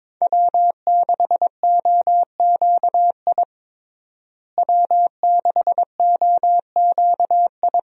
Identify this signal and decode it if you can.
Morse code, heard as a keyed tone, about 22 wpm, 700 Hz. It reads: W6OQI W6OQI
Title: USA Call Signs - 5 Characters - Repeated 3x - 22wpm